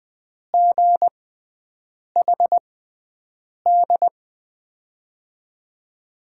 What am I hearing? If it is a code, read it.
Morse code, heard as a keyed tone, about 21 wpm, spaced as for 7 wpm, 700 Hz. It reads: GHD
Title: USA Call Signs - 5 Characters - 6x Character Spacing - 20wpm